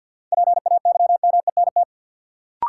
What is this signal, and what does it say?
Morse code, heard as a keyed tone, about 53 wpm, 700 Hz. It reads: PAYMENT
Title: Sentences from Top 1200 Words - New Word Drill Alt - 50wpm